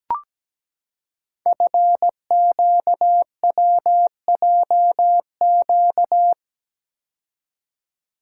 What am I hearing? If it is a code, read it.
Morse code, heard as a keyed tone, about 17 wpm, 700 Hz. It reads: FQWJQ
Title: Code Groups - 5 Letters - 17wpm